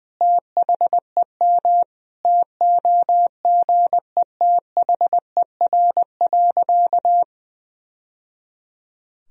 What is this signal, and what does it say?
Morse code, heard as a keyed tone, about 20 wpm, 700 Hz. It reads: THEM TOGETHER.
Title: Sentences from Top 1000 Words - Review Part 3 - 20wpm